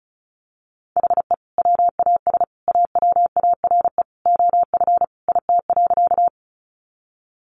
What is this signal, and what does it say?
Morse code, heard as a keyed tone, about 34 wpm, 700 Hz. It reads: HE WAS AWARE OF IT.
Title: Sentences from Top 1200 Words 35wpm